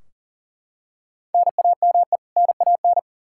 Morse code, heard as a keyed tone, about 40 wpm, 700 Hz. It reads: NAME NAN
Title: Practice QSO - Exchange 1 - Beginning - 40wpm